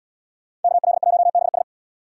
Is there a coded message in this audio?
Yes: Morse code from 0.6 to 1.6 s, heard as a keyed tone, about 77 wpm, 700 Hz. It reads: KL1BS